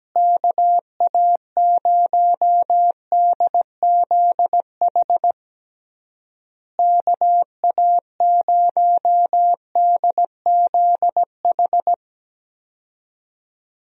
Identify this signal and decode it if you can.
Morse code, heard as a keyed tone, about 17 wpm, 700 Hz. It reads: KA0DZH KA0DZH
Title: USA Call Signs - 6 Characters - Repeated 3x - 17wpm